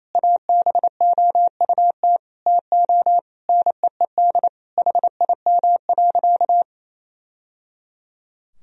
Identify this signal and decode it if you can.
Morse code, heard as a keyed tone, about 28 wpm, 700 Hz. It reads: ABOUT TO NEED HIM.
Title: Sentences from Top 1000 Words - Review Part 1 - 28wpm